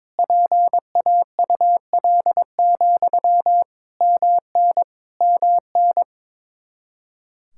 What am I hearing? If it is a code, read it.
Morse code, heard as a keyed tone, about 22 wpm, 700 Hz. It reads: PAUL, MN MN
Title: Practice QSO - Exchange 1 - Beginning - 22wpm